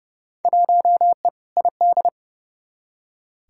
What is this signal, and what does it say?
Morse code, heard as a keyed tone, about 30 wpm, 700 Hz. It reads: 1E ID
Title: Field Day Exchanges - 30wpm